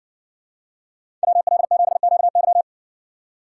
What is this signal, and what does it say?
Morse code, heard as a keyed tone, about 65 wpm, 700 Hz. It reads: WL7QY